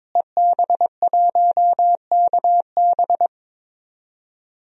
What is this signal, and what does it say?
Morse code, heard as a keyed tone, about 22 wpm, 700 Hz. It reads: EB1KB
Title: DXCC-100 Call Signs - 5 Characters - Repeated 3x - 22wpm